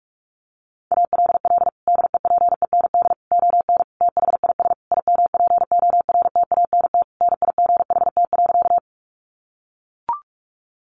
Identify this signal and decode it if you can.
Morse code, heard as a keyed tone, about 44 wpm, 700 Hz. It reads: ALL DEPEND ON THIS IMPORTANT NIGHT.
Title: Sentences from Top 1300 Words 45wpm